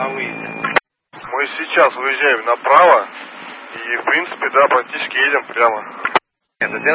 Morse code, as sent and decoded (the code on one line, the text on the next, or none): none